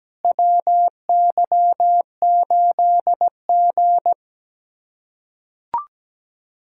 Morse code, heard as a keyed tone, about 17 wpm, 700 Hz. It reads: WY8G
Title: USA Callsign DE Callsign - 4 Characters - 17wpm